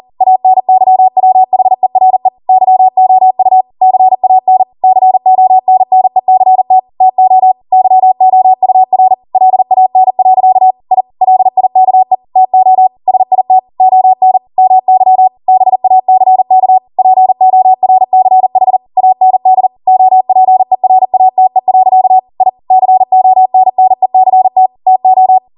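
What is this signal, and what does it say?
Morse code, heard as a keyed tone, about 40 wpm, 750 Hz. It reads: ANYWHERE YOU CAN CONNECT TO YOUR LAN. I LIKE TO SIT ON MY BACK PORCH AND OPERATE. I CONNECT TO
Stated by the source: ARRL 40 WPM Morse code practice transmission.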